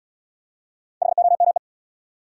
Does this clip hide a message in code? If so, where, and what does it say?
Yes: Morse code from 1.0 to 1.6 s, heard as a keyed tone, about 78 wpm, 700 Hz. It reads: HOME